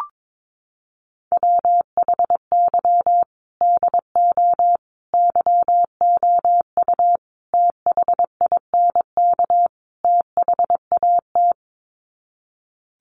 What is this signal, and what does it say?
Morse code, heard as a keyed tone, about 22 wpm, 700 Hz. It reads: WHY DO YOU THINK THAT
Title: Sets of 5 Words Encore2 22wpm